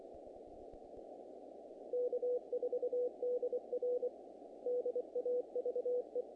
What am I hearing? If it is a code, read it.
Morse code, heard as a keyed tone, about 24 wpm, 490 Hz. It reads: K4DR DAVE
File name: contests / naqp-cw-k5tr-op-k5pi- -1900